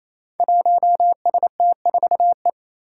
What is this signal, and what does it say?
Morse code, heard as a keyed tone, about 27 wpm, 700 Hz. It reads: 1ST4E
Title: Five Challenging Letter-Number Combinations 28wpm